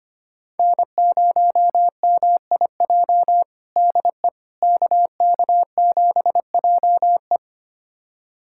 Morse code, heard as a keyed tone, about 25 wpm, 700 Hz. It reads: N0MIJ DE KK7JE
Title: USA Callsign DE Callsign - 5 Characters - 25wpm